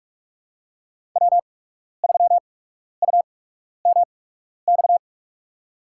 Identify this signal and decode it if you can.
Morse code, heard as a keyed tone, about 44 wpm, spaced as for 12 wpm, 690 Hz. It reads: W3UMX
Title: USA Call Signs - 5 Characters - 8x Character Spacing - 45wpm